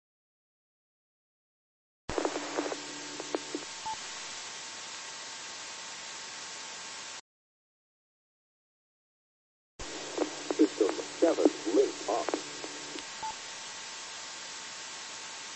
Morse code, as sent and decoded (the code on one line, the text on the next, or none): none